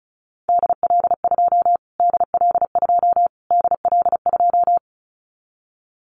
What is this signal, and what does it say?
Morse code, heard as a keyed tone, about 34 wpm, 700 Hz. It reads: DL2 DL2 DL2
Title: Callsign Prefixes for Top 25 DXCC - Repeated 3x - 35wpm